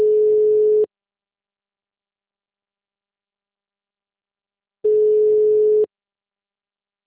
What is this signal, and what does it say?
Ringback tone (Europe, 425 Hz).